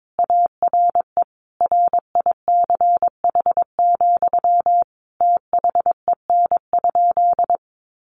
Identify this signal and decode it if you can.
Morse code, heard as a keyed tone, about 22 wpm, 700 Hz. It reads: ARE RICH, THEN?
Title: Sentences from Top 1100 Words 22wpm